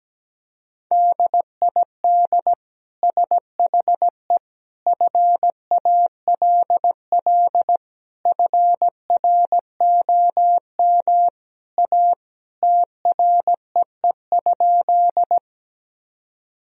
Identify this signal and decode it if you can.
Morse code, heard as a keyed tone, about 17 wpm, 700 Hz. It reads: DID SHE FALL FROM A TREE?
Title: Sentences from Top 1000 Words - Review Part 2 - 17wpm